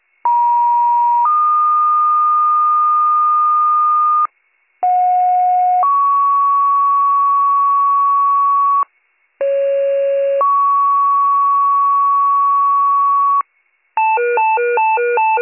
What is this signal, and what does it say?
Two-tone page, heard as a one-second tone, then a three-second tone, about 950 then 1220 Hz, about 730 then 1120 Hz, about 550 then 1120 Hz. It is heard from 0.2 to 13.4 s.